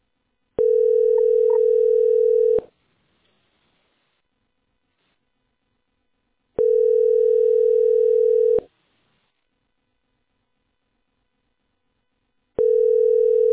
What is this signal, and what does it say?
Ringback tone (North America), 440 + 480 Hz.